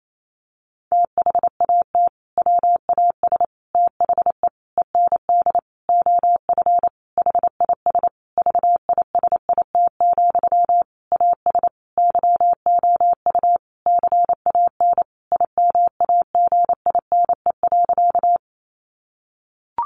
Morse code, heard as a keyed tone, about 28 wpm, 700 Hz. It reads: THAT WAS THE END OF HIS VISIT, AS YOU CAN IMAGINE.
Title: Sentences from Top 1400 Words 28wpm